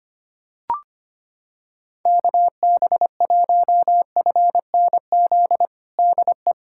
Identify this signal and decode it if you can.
Morse code, heard as a keyed tone, about 25 wpm, 700 Hz. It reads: KB1FNZ DE
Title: USA Callsign DE Callsign - 6 Characters - 25wpm